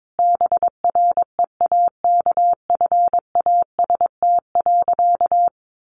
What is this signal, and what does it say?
Morse code, heard as a keyed tone, about 22 wpm, 700 Hz. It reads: BREAKFAST.
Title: Easy Phrases 22wpm